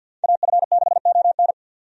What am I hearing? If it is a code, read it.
Morse code, heard as a keyed tone, about 52 wpm, 700 Hz. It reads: AF6ON
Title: USA Call Signs - 5 Characters - 50wpm